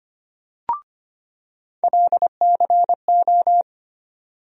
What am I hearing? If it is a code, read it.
Morse code, heard as a keyed tone, about 25 wpm, 700 Hz. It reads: LCO
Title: USA Call Sign Postfixes - 3 Letters 25wpm